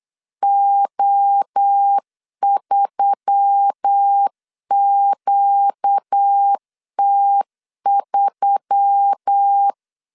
Morse code, heard as a keyed tone, about 8 wpm, 800 Hz. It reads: O3QT3
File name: CW / DET / 1 6 1500C S B 8wpm 5142